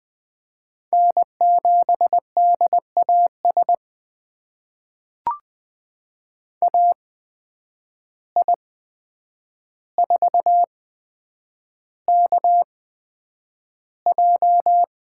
Morse code, heard as a keyed tone, about 20 wpm, spaced as for 9 wpm, 700 Hz. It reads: N7DAS AI4KJ
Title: USA Call Signs - 5 Characters - 8x Character Spacing - 20wpm